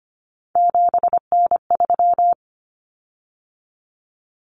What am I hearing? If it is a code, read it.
Morse code, heard as a keyed tone, about 25 wpm, 700 Hz. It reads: 7N3
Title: Common Cut Numbers - 3-Digit 25wpm